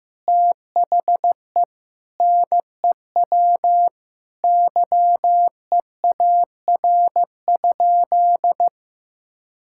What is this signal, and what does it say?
Morse code, heard as a keyed tone, about 16 wpm, 700 Hz. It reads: THE NEW YEAR?
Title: Sentences from Top 1000 Words - Review Part 5 - 15wpm